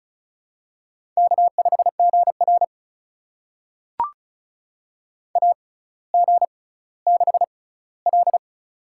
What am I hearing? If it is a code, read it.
Morse code, heard as a keyed tone, about 36 wpm, spaced as for 18 wpm, 700 Hz. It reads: K5GR AG6L
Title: USA Call Signs - 4 Characters - 6x Character Spacing - 35wpm